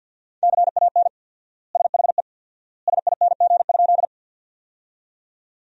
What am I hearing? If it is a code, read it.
Morse code, heard as a keyed tone, about 50 wpm, 700 Hz. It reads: CAN SHE SING?
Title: 4x Word-Spacing - Sentences from Top 400 Words - 50wpm